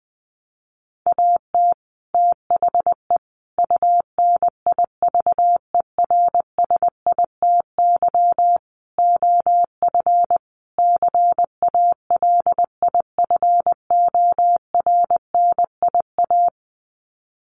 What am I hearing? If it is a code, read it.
Morse code, heard as a keyed tone, about 20 wpm, 700 Hz. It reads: AT THE UNIVERSITY OF CALIFORNIA
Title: Sets of 5 Words Encore2 20wpm